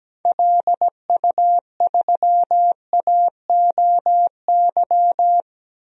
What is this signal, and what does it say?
Morse code, heard as a keyed tone, about 17 wpm, 700 Hz. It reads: LU3AOY
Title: DXCC-100 Call Signs - 6 Characters - Repeated 3x - 17wpm